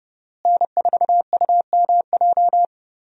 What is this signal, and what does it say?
Morse code, heard as a keyed tone, about 32 wpm, 700 Hz. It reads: N4UMJ
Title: USA Call Signs - 5 Characters - Repeated 2x - 30wpm